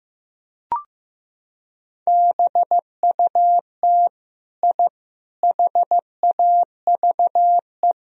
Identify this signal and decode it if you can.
Morse code, heard as a keyed tone, about 15 wpm, 700 Hz. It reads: BUT I HAVE
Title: Sentences from Top 1000 Words - Review Part 4 - 15wpm